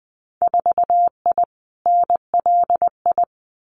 Morse code, heard as a keyed tone, about 20 wpm, 700 Hz. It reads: 4I NLI
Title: Winter Field Day Exchanges Encore 20wpm